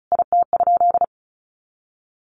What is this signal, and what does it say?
Morse code, heard as a keyed tone, about 34 wpm, 700 Hz. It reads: IT?
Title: Sentences from Top 1500 Words 35wpm